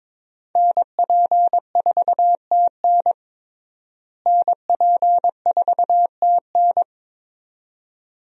Morse code, heard as a keyed tone, about 22 wpm, 700 Hz. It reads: NP4TN NP4TN
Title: USA Call Signs - 5 Characters - Repeated 2x - 22wpm